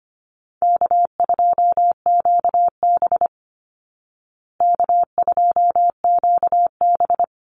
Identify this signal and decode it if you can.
Morse code, heard as a keyed tone, about 25 wpm, 700 Hz. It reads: K2QB K2QB
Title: Easy Call Signs Repeated 3x - 25wpm